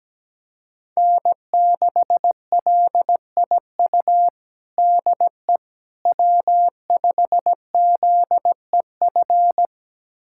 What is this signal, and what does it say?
Morse code, heard as a keyed tone, about 17 wpm, 700 Hz. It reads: N6LIU DE W5ZEF
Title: USA Callsign DE Callsign - 5 Characters - 17wpm